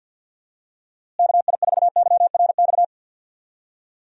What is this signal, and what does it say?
Morse code, heard as a keyed tone, about 45 wpm, 700 Hz. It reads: KI4YRX
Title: USA Call Signs - 6 Characters - Repeated 2x - 50wpm